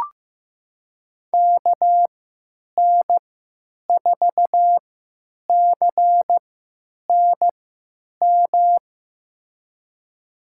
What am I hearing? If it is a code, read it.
Morse code, heard as a keyed tone, about 15 wpm, spaced as for 9 wpm, 700 Hz. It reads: KN4CNM
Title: USA Call Signs - 6 Characters - 3x Character Spacing - 15wpm